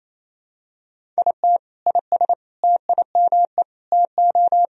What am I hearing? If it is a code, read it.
Morse code, heard as a keyed tone, about 28 wpm, 700 Hz. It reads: IT IS TIME TO
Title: Sentences from Top 1000 Words - Review Part 1 - 28wpm